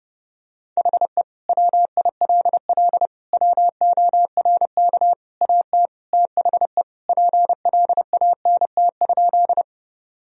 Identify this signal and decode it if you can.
Morse code, heard as a keyed tone, about 30 wpm, 700 Hz. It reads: HE WILL WORK AT THE PLANT?
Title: Sentences from Top 1000 Words - Review Part 1 - 30wpm